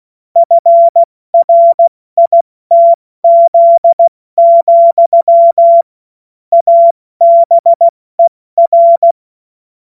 Morse code, heard as a keyed tone, about 16 wpm, 680 Hz. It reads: FRITZ, ABER
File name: Morsekurs / Witze / Witz 17. August 2021, 70 F80